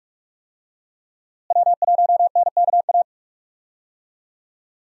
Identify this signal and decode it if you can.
Morse code, heard as a keyed tone, about 44 wpm, 700 Hz. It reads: W1NKA
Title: USA Call Signs - 5 Characters - Repeated 3x - 45wpm